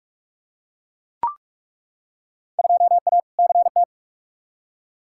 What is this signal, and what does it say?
Morse code, heard as a keyed tone, about 43 wpm, 700 Hz. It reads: 2A CT
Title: Field Day Exchanges - 45wpm